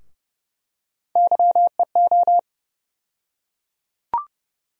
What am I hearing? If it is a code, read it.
Morse code, heard as a keyed tone, about 32 wpm, 700 Hz. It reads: YEO
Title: USA Call Sign Postfixes - 3 Letters 30wpm